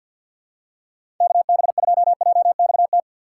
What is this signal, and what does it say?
Morse code, heard as a keyed tone, about 51 wpm, 700 Hz. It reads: KB2JXT
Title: USA Call Signs - 2x Character Spacing - 50wpm